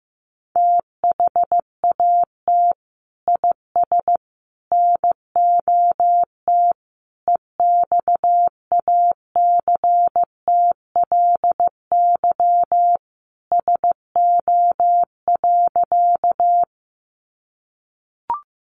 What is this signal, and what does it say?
Morse code, heard as a keyed tone, about 15 wpm, 700 Hz. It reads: THAT IS NOT EXACTLY SO.
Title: Sentences from Top 900 Words 15wpm